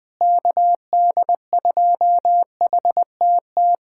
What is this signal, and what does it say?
Morse code, heard as a keyed tone, about 20 wpm, 700 Hz. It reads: KD2HTT
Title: USA Callsign DE Callsign - 6 Characters - 20wpm